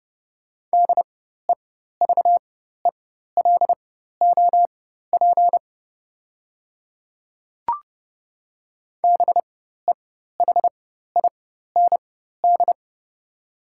Morse code, heard as a keyed tone, about 30 wpm, spaced as for 14 wpm, 690 Hz. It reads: DEVELOP BEHIND
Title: 4x Character Spacing - Top 500 Words - 30wpm